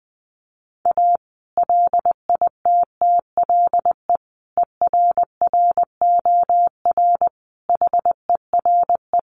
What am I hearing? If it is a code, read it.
Morse code, heard as a keyed tone, about 20 wpm, 700 Hz. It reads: A LITTLE ERROR HERE
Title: A Sound Of Thunder 20wpm